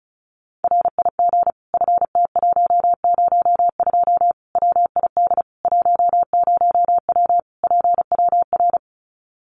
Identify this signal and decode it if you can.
Morse code, heard as a keyed tone, about 34 wpm, 700 Hz. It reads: RIG FT102 WID 10W PWR
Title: Common QSO Phrases 35wpm